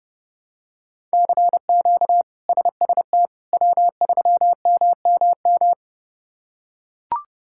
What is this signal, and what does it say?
Morse code, heard as a keyed tone, about 29 wpm, 700 Hz. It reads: CQ SST W3MMM
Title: K1USN SST QSO Practice - 30wpm